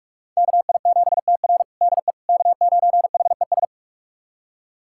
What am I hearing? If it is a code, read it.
Morse code, heard as a keyed tone, about 46 wpm, 700 Hz. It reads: KI7TR DE K9HES
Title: USA Callsign DE Callsign - 5 Characters - 45wpm